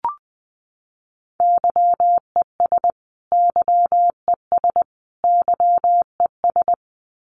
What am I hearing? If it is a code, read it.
Morse code, heard as a keyed tone, about 19 wpm, 700 Hz. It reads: YES YES YES
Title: Top 500 Words Repeated 20wpm